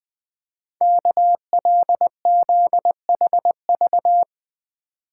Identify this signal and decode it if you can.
Morse code, heard as a keyed tone, about 20 wpm, 700 Hz. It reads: KLZHV
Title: Code Groups - 5 Letters - 20wpm